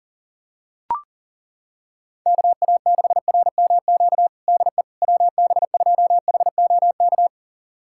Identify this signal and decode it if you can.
Morse code, heard as a keyed tone, about 40 wpm, 700 Hz. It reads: KA6RMQ DE WB2HOK
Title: USA Callsign DE Callsign - 6 Characters - 40wpm